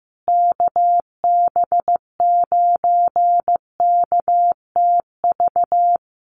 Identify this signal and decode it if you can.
Morse code, heard as a keyed tone, about 15 wpm, 700 Hz. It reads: KB9KTV
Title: USA Call Signs Repeated 2x 15wpm